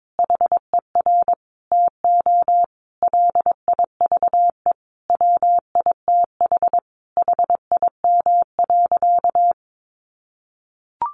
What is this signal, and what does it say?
Morse code, heard as a keyed tone, about 22 wpm, 700 Hz. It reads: HER TO LIVE WITH HIM.
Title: Sentences from Top 1100 Words 22wpm